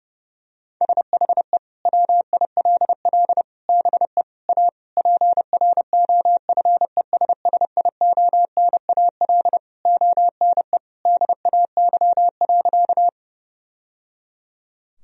Morse code, heard as a keyed tone, about 30 wpm, 700 Hz. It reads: SHE WILL BE A PROFESSIONAL ONE DAY.
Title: Sentences from Top 1000 Words - Review Part 4 - 30wpm